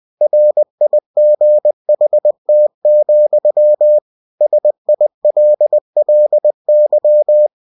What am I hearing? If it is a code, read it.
Morse code, heard as a keyed tone, about 20 wpm, 600 Hz. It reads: RIGHT, SILLY